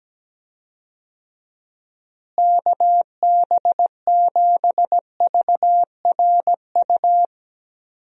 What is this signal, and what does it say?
Morse code, heard as a keyed tone, about 17 wpm, 700 Hz. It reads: KB7VRU
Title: USA Call Signs - 6 Characters - Repeated 2x - 17wpm